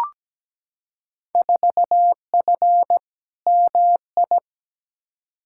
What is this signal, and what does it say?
Morse code, heard as a keyed tone, about 17 wpm, 700 Hz. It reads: 4F MI
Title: Field Day Exchanges Encore - 17wpm